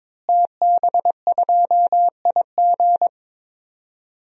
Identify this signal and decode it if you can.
Morse code, heard as a keyed tone, about 22 wpm, 700 Hz. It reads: TB2IG
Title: DXCC-100 Call Signs - 5 Characters - Repeated 3x - 22wpm